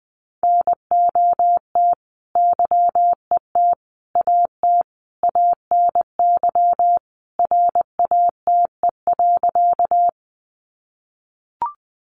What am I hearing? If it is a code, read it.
Morse code, heard as a keyed tone, about 20 wpm, 700 Hz. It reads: NOT YET AT ANY RATE.
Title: Sentences from Top 700 Words 20wpm